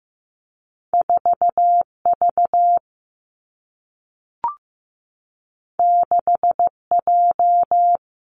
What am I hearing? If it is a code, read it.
Morse code, heard as a keyed tone, about 15 wpm, 700 Hz. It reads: 4 V 6 J
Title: Two Challenging Letter-Number Combinations 15wpm